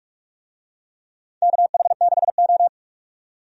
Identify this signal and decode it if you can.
Morse code, heard as a keyed tone, about 44 wpm, 700 Hz. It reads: KH6O
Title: DXCC-100 Call Signs - 4 Characters - 45wpm